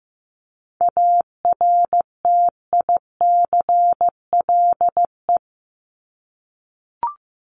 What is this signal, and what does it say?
Morse code, heard as a keyed tone, about 15 wpm, 700 Hz. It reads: ARTICLE
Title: Top 751-1000 Words 15wpm